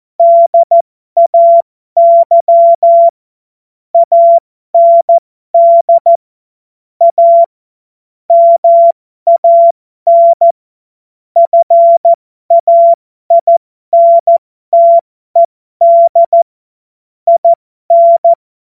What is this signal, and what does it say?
Morse code, heard as a keyed tone, about 14 wpm, spaced as for 12 wpm, 680 Hz. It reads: DAY AND A MAN FAINTED IN
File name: Morsekurs / Witze